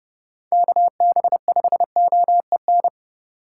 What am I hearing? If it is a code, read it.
Morse code, heard as a keyed tone, about 30 wpm, 700 Hz. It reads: KB5OEN
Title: USA Call Signs - 6 Characters - 4x Character Spacing - 30wpm